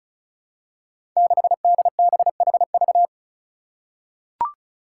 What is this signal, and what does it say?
Morse code, heard as a keyed tone, about 36 wpm, 700 Hz. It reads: 6DBHV
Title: Five Challenging Letter-Number Combinations 35wpm